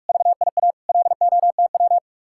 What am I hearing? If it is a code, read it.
Morse code, heard as a keyed tone, about 59 wpm, spaced as for 52 wpm, 700 Hz. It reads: VIA LOTW
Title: Speed-Racing Common QSO Phrases - 45wpm